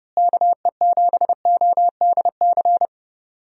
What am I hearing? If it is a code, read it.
Morse code, heard as a keyed tone, about 30 wpm, 700 Hz. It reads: KE7ODC
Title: USA Call Signs - 6 Characters - Repeated 3x - 30wpm